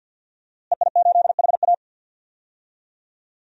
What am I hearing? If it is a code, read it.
Morse code, heard as a keyed tone, about 47 wpm, 700 Hz. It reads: EI8HA
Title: DXCC-100 Call Signs - 5 Characters - Repeated 3x - 50wpm